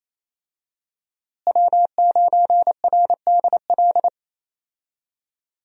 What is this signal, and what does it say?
Morse code, heard as a keyed tone, about 28 wpm, 700 Hz. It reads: W9RDL
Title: USA Call Signs - 5 Characters - Repeated 2x - 28wpm